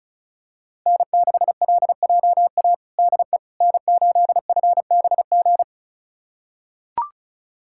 Morse code, heard as a keyed tone, about 36 wpm, 700 Hz. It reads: N6LJA DE N8FBG
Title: USA Callsign DE Callsign - 5 Characters - 35wpm